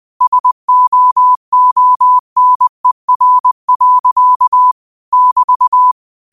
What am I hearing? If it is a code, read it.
Morse code, heard as a keyed tone, about 20 wpm, 1000 Hz. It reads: SOONER. <BT>